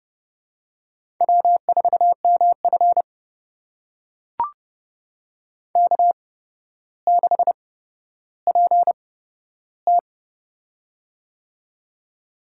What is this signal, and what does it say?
Morse code, heard as a keyed tone, about 30 wpm, spaced as for 13 wpm, 700 Hz. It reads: W4MF K6PT
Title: USA Call Signs - 4 Characters - 8x Character Spacing - 30wpm